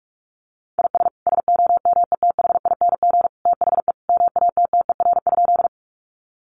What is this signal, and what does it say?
Morse code, heard as a keyed tone, about 44 wpm, 700 Hz. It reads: IS SOMETHING THE MATTER?
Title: Sentences from Top 700 Words 45wpm